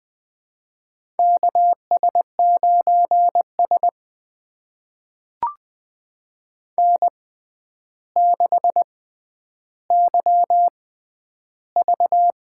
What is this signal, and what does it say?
Morse code, heard as a keyed tone, about 20 wpm, spaced as for 10 wpm, 700 Hz. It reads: KS9S N6YV
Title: USA Call Signs - 4 Characters - 6x Character Spacing - 20wpm